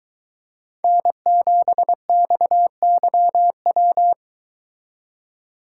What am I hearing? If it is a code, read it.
Morse code, heard as a keyed tone, about 22 wpm, 700 Hz. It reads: N7XYW